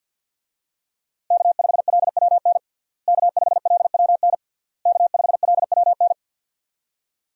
Morse code, heard as a keyed tone, about 52 wpm, 700 Hz. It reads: K5LWN K5LWN K5LWN
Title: USA Call Signs - 5 Characters - Repeated 3x - 50wpm